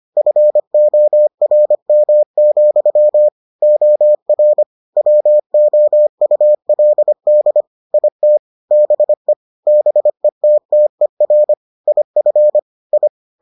Morse code, heard as a keyed tone, about 25 wpm, 600 Hz. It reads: FORM, OR WOULD IT BE BETTER IF I